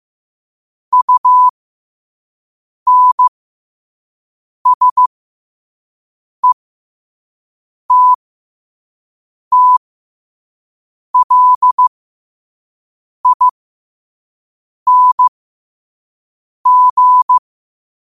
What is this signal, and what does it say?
Morse code, heard as a keyed tone, about 15 wpm, spaced as for 5 wpm, 1000 Hz. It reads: UNSETTLING
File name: qotdfiles / 5wpm